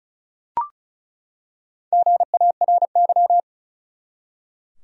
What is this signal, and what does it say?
Morse code, heard as a keyed tone, about 36 wpm, 700 Hz. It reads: GARY
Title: USA Common Names - Max 4 Letters - 35wpm